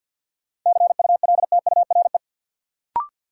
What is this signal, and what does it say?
Morse code, heard as a keyed tone, about 52 wpm, 700 Hz. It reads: CULTURE
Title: Sentences from Top 1000 Words - Review Part 6 - New Word Drill - 50wpm